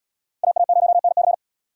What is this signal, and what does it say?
Morse code, heard as a keyed tone, about 77 wpm, 700 Hz. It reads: AA0NC